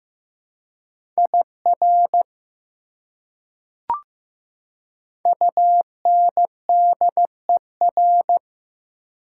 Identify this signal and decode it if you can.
Morse code, heard as a keyed tone, about 15 wpm, 700 Hz. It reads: IR UNDER
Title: Prefixes and Suffixes 15wpm